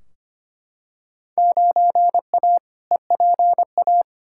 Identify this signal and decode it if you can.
Morse code, heard as a keyed tone, about 25 wpm, 700 Hz. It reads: 9A EPA
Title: Field Day Exchanges Encore - 25wpm